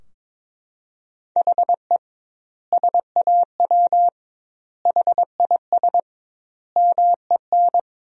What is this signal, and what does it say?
Morse code, heard as a keyed tone, about 22 wpm, 700 Hz. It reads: HE SAW HIS MEN
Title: 2x Word-Spacing - Sentences from Top 300 Words - 22wpm